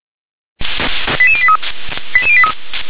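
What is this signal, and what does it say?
Five-tone selective call (ZVEI), heard as quick runs of five tones, tones of about 70 ms. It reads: 88083 88083
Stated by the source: ZVEI five-tone selective call 88741 received with an SDR.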